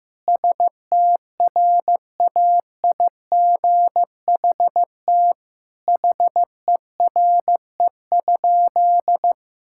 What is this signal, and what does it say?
Morse code, heard as a keyed tone, about 15 wpm, 700 Hz. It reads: STRAIGHT HERE?
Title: Sentences from Top 1000 Words - Review Part 6 - 15wpm